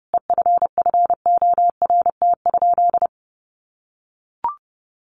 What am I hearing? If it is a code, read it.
Morse code, heard as a keyed tone, about 30 wpm, 700 Hz. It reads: EFFORT?
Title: Sentences from Top 1200 Words 30wpm